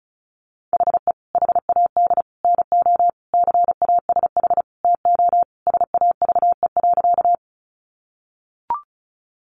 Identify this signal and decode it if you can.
Morse code, heard as a keyed tone, about 34 wpm, 700 Hz. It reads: HE HAD NO CASH TO SAVE.
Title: Sentences from Top 1100 Words 35wpm